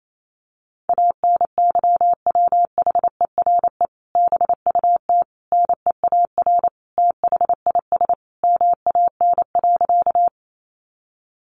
Morse code, heard as a keyed tone, about 28 wpm, 700 Hz. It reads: ANYWHERE BUT NEAR THIS MAN.
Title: Sentences from Top 1400 Words 28wpm